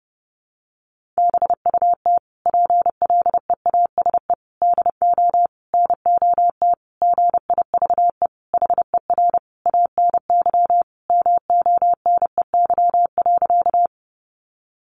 Morse code, heard as a keyed tone, about 30 wpm, 700 Hz. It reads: BUT PLEASE DO NOT GIVE HER ANY MONEY.
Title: Sentences from Top 1500 Words 30wpm